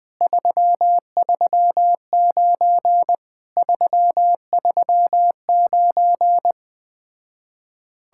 Morse code, heard as a keyed tone, about 20 wpm, 700 Hz. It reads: 339 339
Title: Speed-Racing Common QSO Phrases - 17wpm